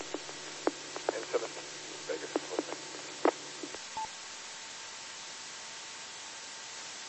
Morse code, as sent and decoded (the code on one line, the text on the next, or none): none